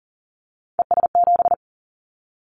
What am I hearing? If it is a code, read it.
Morse code, heard as a keyed tone, about 39 wpm, 700 Hz. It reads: ES7
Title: Callsign Prefixes for Top 50 DXCC - 40wpm